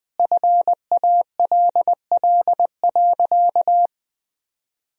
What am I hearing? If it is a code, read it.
Morse code, heard as a keyed tone, about 20 wpm, 700 Hz. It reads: FALL.
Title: 4x Word-Spacing - Sentences from Top 500 Words - 20wpm